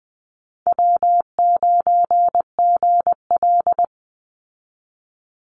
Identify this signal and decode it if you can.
Morse code, heard as a keyed tone, about 20 wpm, 700 Hz. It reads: W9GL
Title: Easy Call Signs Repeated 3x - 20wpm